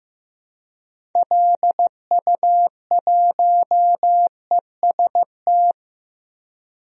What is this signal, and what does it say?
Morse code, heard as a keyed tone, about 15 wpm, 700 Hz. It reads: LU1EST
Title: DXCC-100 Call Signs - 6 Characters - Repeated 3x - 15wpm